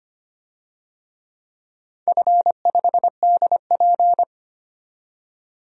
Morse code, heard as a keyed tone, about 25 wpm, 700 Hz. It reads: F5DP
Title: DXCC-100 Call Signs - 4 Characters - 25wpm